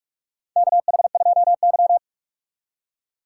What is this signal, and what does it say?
Morse code, heard as a keyed tone, about 44 wpm, 700 Hz. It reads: KH2Y
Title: USA Call Signs - 3x Character Spacing - 45wpm